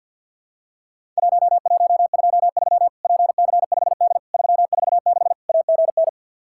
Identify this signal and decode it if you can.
Morse code, heard as a keyed tone, about 50 wpm, 700 Hz. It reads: 1123 PC5D 346 AGN
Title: Worked All Europe DX Contest - QTC Exchange with Repeats - 50wpm